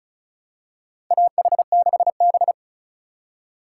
Morse code, heard as a keyed tone, about 36 wpm, 700 Hz. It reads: AH6B
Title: USA Call Signs - 4 Characters - 2x Character Spacing - 35wpm